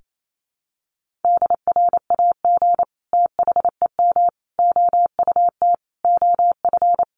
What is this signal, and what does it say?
Morse code, heard as a keyed tone, about 28 wpm, 700 Hz. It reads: DRAG THEM OUT OF
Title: Sentences from Top 1500 Words 28wpm